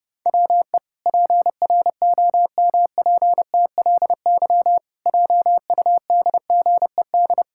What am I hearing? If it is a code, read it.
Morse code, heard as a keyed tone, about 32 wpm, 700 Hz. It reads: WE PROMPTLY JUDGED
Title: Pangram Challenge v2 - 30wpm